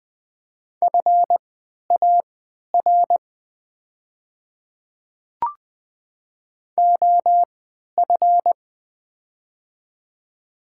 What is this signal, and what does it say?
Morse code, heard as a keyed tone, about 20 wpm, spaced as for 11 wpm, 690 Hz. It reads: FAR OF
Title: 3x Character Spacing - Top 300 Words - 20wpm